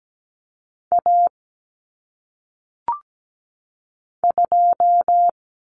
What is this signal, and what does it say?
Morse code, heard as a keyed tone, about 17 wpm, 700 Hz. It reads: A 2
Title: Common Cut Numbers - 1-Digit 17wpm